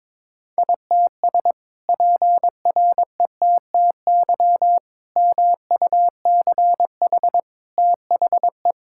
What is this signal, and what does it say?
Morse code, heard as a keyed tone, about 22 wpm, 700 Hz. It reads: ITS PRETTY MUCH THE
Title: Sentences from Top 1000 Words - Review Part 6 - 22wpm